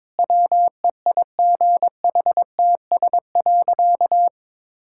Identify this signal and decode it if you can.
Morse code, heard as a keyed tone, about 22 wpm, 700 Hz. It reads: WEIGHTS.
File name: practice / 22wpm / Easy Sentences with Hints - 22wpm